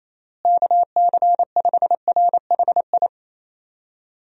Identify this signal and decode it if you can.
Morse code, heard as a keyed tone, about 28 wpm, 700 Hz. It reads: KC5RHI
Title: USA Call Signs - 6 Characters - Repeated 3x - 28wpm